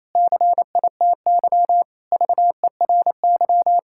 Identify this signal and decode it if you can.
Morse code, heard as a keyed tone, about 28 wpm, 700 Hz. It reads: CITY VERY
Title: Sentences from Top 1000 Words - Review Part 6 - 28wpm